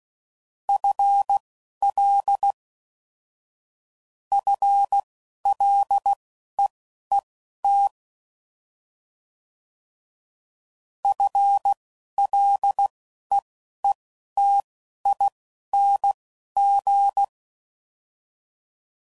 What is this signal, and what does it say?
Morse code, heard as a keyed tone, about 16 wpm, spaced as for 11 wpm, 800 Hz. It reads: FL FLEET FLEETING
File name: Basic Sets / QST de WA7RSO / Combnations